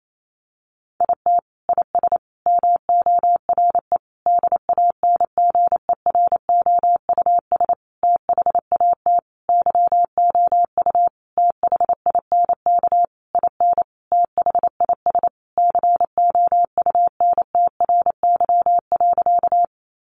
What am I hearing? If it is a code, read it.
Morse code, heard as a keyed tone, about 28 wpm, 700 Hz. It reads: IT IS MORE DANGEROUS THAT YOU THINK IN THIS COUNTRY.
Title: Sentences from Top 1200 Words 28wpm